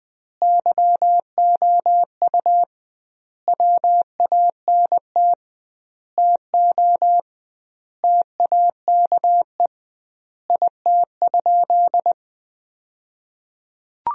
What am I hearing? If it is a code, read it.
Morse code, heard as a keyed tone, about 19 wpm, 700 Hz. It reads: YOU WANT TO TAKE IT?
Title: 2x Word-Spacing - Sentences from Top 300 Words - 20wpm